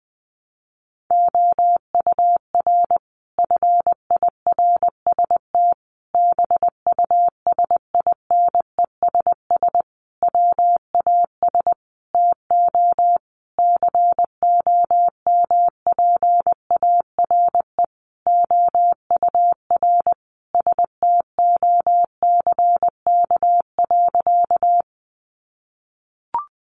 Morse code, heard as a keyed tone, about 20 wpm, 700 Hz. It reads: OUR FIRST BUSINESS WAS TO COMPARE OUR STOCK.
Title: Sentences from Top 1300 Words 20wpm